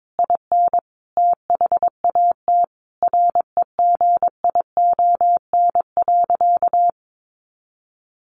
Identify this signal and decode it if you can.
Morse code, heard as a keyed tone, about 22 wpm, 700 Hz. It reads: IN THAT REGION.
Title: A Sound Of Thunder 22wpm